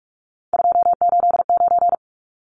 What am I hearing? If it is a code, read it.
Morse code, heard as a keyed tone, about 44 wpm, 700 Hz. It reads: 289
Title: Numbers - 3-Digit 45wpm